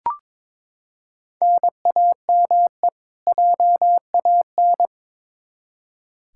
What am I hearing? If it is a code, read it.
Morse code, heard as a keyed tone, about 22 wpm, 700 Hz. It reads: NAME JAN
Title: Practice QSO - Exchange 1 - Beginning - 22wpm